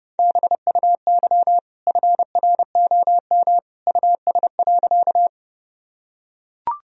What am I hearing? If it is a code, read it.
Morse code, heard as a keyed tone, about 30 wpm, 700 Hz. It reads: BUY FROM US.
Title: Sentences from Top 1000 Words - Review Part 4 - 30wpm